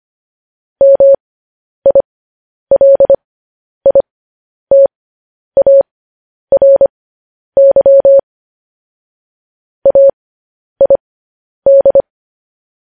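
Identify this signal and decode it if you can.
Morse code, heard as a keyed tone, about 25 wpm, spaced as for 10 wpm, 550 Hz. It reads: MILITARY AID
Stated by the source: News headlines sent in Morse code at 10 wpm.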